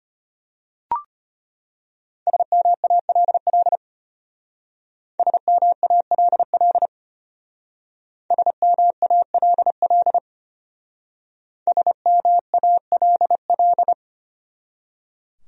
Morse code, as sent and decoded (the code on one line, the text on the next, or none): ... -- .- .-.. .-.. / ... -- .- .-.. .-.. / ... -- .- .-.. .-.. / ... -- .- .-.. .-..
SMALL SMALL SMALL SMALL